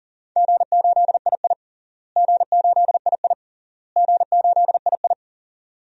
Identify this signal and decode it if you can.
Morse code, heard as a keyed tone, about 40 wpm, 700 Hz. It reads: G8II G8II G8II
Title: DXCC-100 Call Signs - 4 Characters - Repeated 3x - 40wpm